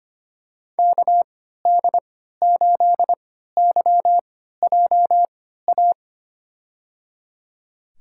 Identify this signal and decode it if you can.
Morse code, heard as a keyed tone, about 25 wpm, spaced as for 14 wpm, 700 Hz. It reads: KD8YJA
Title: USA Call Signs - 6 Characters - 3x Character Spacing - 25wpm